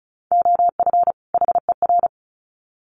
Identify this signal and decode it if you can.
Morse code, heard as a keyed tone, about 34 wpm, 700 Hz. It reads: OF HER
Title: Sets of 2 Words 35wpm